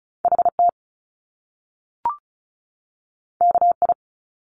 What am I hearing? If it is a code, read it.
Morse code, heard as a keyed tone, about 36 wpm, 700 Hz. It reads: HT KI
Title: Top Two Letter Combinations in Words 35wpm